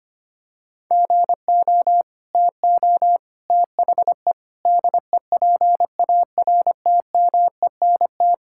Morse code, heard as a keyed tone, about 25 wpm, 700 Hz. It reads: GO TO THE DEPARTMENT
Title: Sentences from Top 1000 Words - Review Part 5 - 25wpm